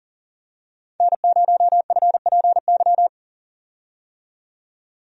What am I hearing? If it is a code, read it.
Morse code, heard as a keyed tone, about 40 wpm, 700 Hz. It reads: N0FPY
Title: USA Call Signs - 5 Characters - 40wpm